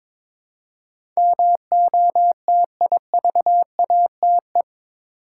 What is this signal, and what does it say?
Morse code, heard as a keyed tone, about 22 wpm, 700 Hz. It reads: MOTIVATE
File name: practice / 22wpm / Eight Letter Words - 22wpm